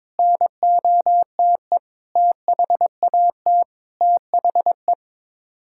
Morse code, heard as a keyed tone, about 22 wpm, 700 Hz. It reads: NOTE THAT THE
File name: practice / 22wpm / Sets of 8 Words - 22wpm